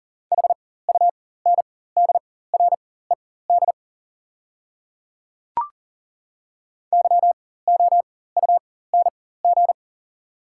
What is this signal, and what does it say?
Morse code, heard as a keyed tone, about 40 wpm, spaced as for 19 wpm, 690 Hz. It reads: HUNDRED YOUNG
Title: 4x Character Spacing - Top 500 Words - 40wpm